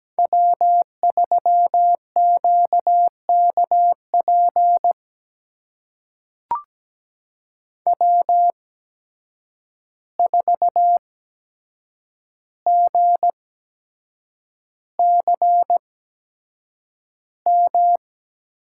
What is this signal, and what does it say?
Morse code, heard as a keyed tone, about 17 wpm, spaced as for 7 wpm, 700 Hz. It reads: W3QKP W4GCM
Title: USA Call Signs - 8x Character Spacing - 17wpm